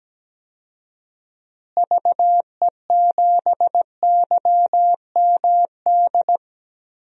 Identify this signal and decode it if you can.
Morse code, heard as a keyed tone, about 17 wpm, 700 Hz. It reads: VE7YMD
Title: DXCC-100 Call Signs - 6 Characters - Repeated 2x - 17wpm